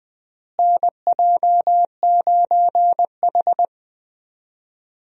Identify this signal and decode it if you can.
Morse code, heard as a keyed tone, about 20 wpm, 700 Hz. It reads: NJ9H
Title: USA Call Signs - 4 Characters - Repeated 3x - 20wpm